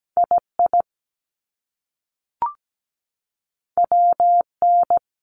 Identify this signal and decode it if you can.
Morse code, heard as a keyed tone, about 17 wpm, 700 Hz. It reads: II WN
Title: Top Two Letter Combinations in Words 17wpm